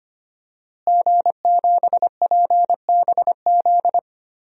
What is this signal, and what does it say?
Morse code, heard as a keyed tone, about 25 wpm, 700 Hz. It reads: G7PBZ
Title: DXCC-100 Call Signs - 5 Characters - Repeated 3x - 25wpm